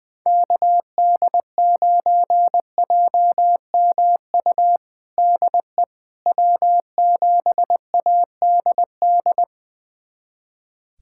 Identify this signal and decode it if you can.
Morse code, heard as a keyed tone, about 20 wpm, 700 Hz. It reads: KD9JMU DE W7ADD
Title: USA Callsign DE Callsign 20wpm